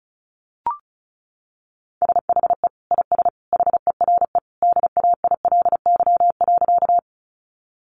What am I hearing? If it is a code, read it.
Morse code, heard as a keyed tone, about 34 wpm, 700 Hz. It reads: SHE IS HERE DAILY.
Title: Sentences from Top 1100 Words 35wpm